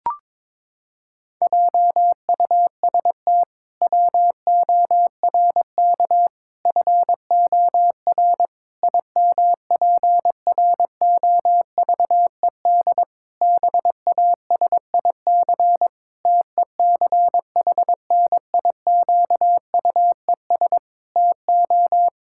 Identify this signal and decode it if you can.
Morse code, heard as a keyed tone, about 22 wpm, 700 Hz. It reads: JUST WORK FOR IMPROVED BASIC TECHNIQUES TO
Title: Pangram Challenge v2 - 22wpm